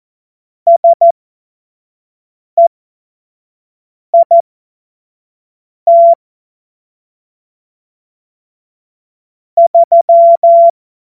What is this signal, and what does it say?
Morse code, heard as a keyed tone, about 14 wpm, spaced as for 5 wpm, 680 Hz. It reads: SEIT 3
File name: Morsekurs/Witze